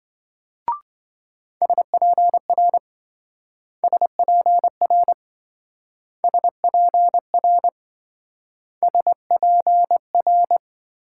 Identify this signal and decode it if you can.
Morse code, heard as a keyed tone, about 27 wpm, 700 Hz. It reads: SPR SPR SPR SPR